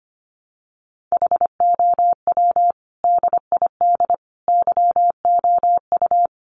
Morse code, heard as a keyed tone, about 25 wpm, 700 Hz. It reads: HOW DID YOU
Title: Sentences from Top 1100 Words 25wpm